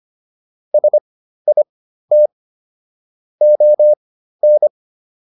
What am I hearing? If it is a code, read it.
Morse code, heard as a keyed tone, about 25 wpm, spaced as for 13 wpm, 600 Hz. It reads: SIT ON